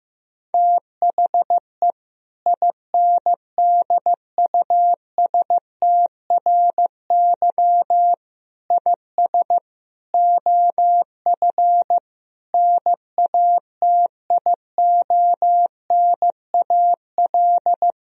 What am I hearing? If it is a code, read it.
Morse code, heard as a keyed tone, about 16 wpm, 700 Hz. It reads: THE INDUSTRY IS OF NATIONAL
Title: Sentences from Top 1000 Words - Review Part 4 - 15wpm